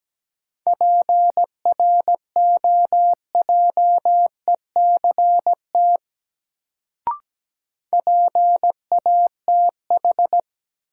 Morse code, heard as a keyed tone, about 17 wpm, 700 Hz. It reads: PROJECT PATH
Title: Sentences from Top 1400 Words - New Word Drill Alt - 17wpm